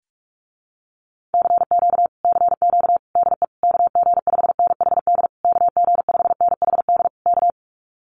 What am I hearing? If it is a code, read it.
Morse code, heard as a keyed tone, about 44 wpm, 700 Hz. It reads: CQ CQ DE KG5NHD KG5NHD K
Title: Common QSO Phrases Encore 45wpm